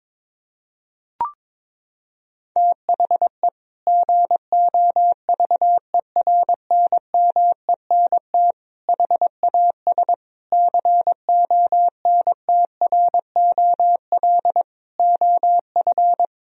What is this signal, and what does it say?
Morse code, heard as a keyed tone, about 22 wpm, 700 Hz. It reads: THE GOVERNMENT HAS CONTROL OF
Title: Sentences from Top 1000 Words - Review Part 3 - 22wpm